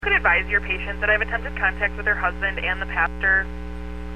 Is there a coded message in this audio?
Yes: Emergency Alert System (SAME) tones from 0.0 to 3.1 s, heard as harsh buzzing bursts, with speech.